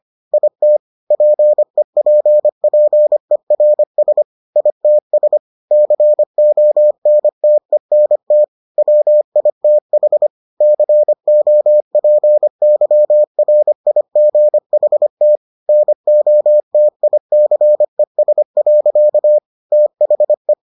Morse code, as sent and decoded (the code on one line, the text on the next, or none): .. - / .--. .--. .--. .-. ... / .. - ... / -.-. --- -. - . -. - / .-- .. - .... / -.-. --- .--. -.-- .-. .. --. .... - / -. --- - .. -.-. ... .-.-.- / - .... .
IT PEPPERS ITS CONTENT WITH COPYRIGHT NOTICES. THE